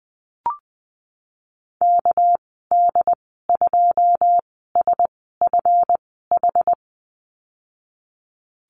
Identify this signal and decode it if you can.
Morse code, heard as a keyed tone, about 20 wpm, spaced as for 15 wpm, 700 Hz. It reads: KD2SFH
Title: 207 Lesson 33g - Letter j - Call Signs 20wpm